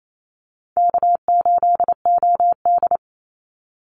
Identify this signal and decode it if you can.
Morse code, heard as a keyed tone, about 28 wpm, 700 Hz. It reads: K8OD
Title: Easy Call Signs - 28wpm